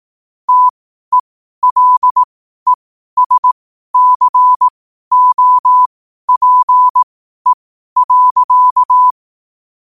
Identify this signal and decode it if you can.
Morse code, heard as a keyed tone, about 18 wpm, spaced as for 12 wpm, 1000 Hz. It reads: TELESCOPE.
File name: qotdfiles / 12wpm